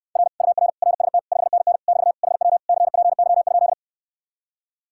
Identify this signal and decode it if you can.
Morse code, heard as a keyed tone, about 69 wpm, 700 Hz. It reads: R UR RST 5NN <BT> HW CPY?